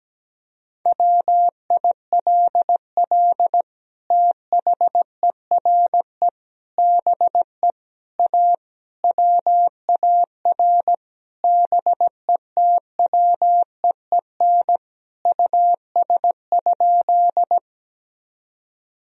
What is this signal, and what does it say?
Morse code, heard as a keyed tone, about 17 wpm, 700 Hz. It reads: WILL THERE BE A WAR BETWEEN US?
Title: Sentences from Top 1000 Words - Review Part 2 - 17wpm